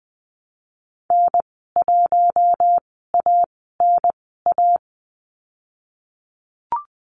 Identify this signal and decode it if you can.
Morse code, heard as a keyed tone, about 20 wpm, spaced as for 14 wpm, 700 Hz. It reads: N1ANA
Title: 042 Lesson 8g - Number 1 - Call Signs 20wpm